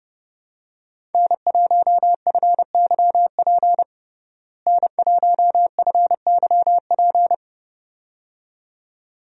Morse code, heard as a keyed tone, about 30 wpm, 700 Hz. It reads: N1FYP N1FYP
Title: USA Call Signs - 5 Characters - Repeated 2x - 30wpm